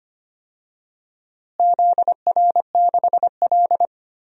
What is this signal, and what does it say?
Morse code, heard as a keyed tone, about 25 wpm, 700 Hz. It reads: ZR6L
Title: DXCC-100 Call Signs - 4 Characters - Repeated 2x - 25wpm